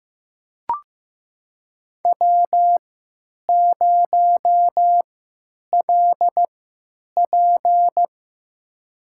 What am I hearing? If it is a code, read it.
Morse code, heard as a keyed tone, about 15 wpm, spaced as for 9 wpm, 700 Hz. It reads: W0LP
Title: USA Call Signs - 3x Character Spacing - 15wpm